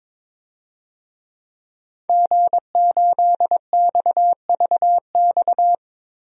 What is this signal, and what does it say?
Morse code, heard as a keyed tone, about 22 wpm, 700 Hz. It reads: G8XVX
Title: DXCC-100 Call Signs - 5 Characters - Repeated 2x - 22wpm